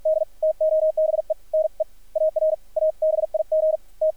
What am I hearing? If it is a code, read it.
Morse code, heard as a keyed tone, about 53 wpm, spaced as for 40 wpm, 590 Hz, with radio noise. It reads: D TOBE NEAR A BIG T
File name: archived / CRSnet